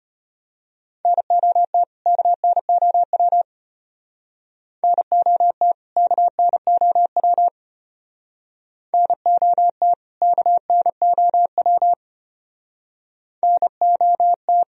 Morse code, heard as a keyed tone, about 32 wpm, 700 Hz. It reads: NOT KNOW NOT KNOW NOT KNOW NOT